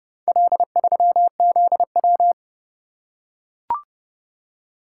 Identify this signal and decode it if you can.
Morse code, heard as a keyed tone, about 29 wpm, 700 Hz. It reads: L3ZW